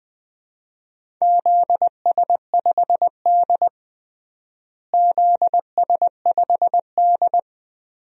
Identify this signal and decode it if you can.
Morse code, heard as a keyed tone, about 21 wpm, 700 Hz. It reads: ZS5D ZS5D
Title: DXCC-100 Call Signs - 4 Characters - Repeated 2x - 20wpm